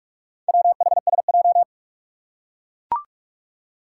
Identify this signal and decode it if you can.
Morse code, heard as a keyed tone, about 42 wpm, 700 Hz. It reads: WHSJ
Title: Four Challenging Letter-Number Combinations 45wpm